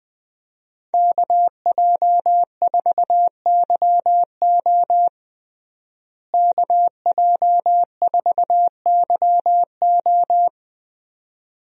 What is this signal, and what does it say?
Morse code, heard as a keyed tone, about 21 wpm, 700 Hz. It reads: KJ4YO KJ4YO
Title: USA Call Signs - 5 Characters - Repeated 2x - 20wpm